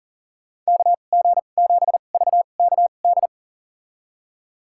Morse code, heard as a keyed tone, about 40 wpm, spaced as for 29 wpm, 700 Hz. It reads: KG7VKD
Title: USA Call Signs - 6 Characters - 2x Character Spacing - 40wpm